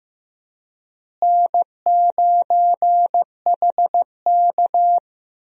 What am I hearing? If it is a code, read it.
Morse code, heard as a keyed tone, about 15 wpm, 700 Hz. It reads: N9HK
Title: USA Call Signs - 4 Characters - Repeated 3x - 15wpm